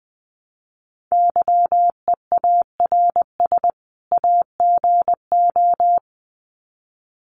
Morse code, heard as a keyed tone, about 20 wpm, 700 Hz. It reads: YEARS AGO
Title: Sets of 2 Words 20wpm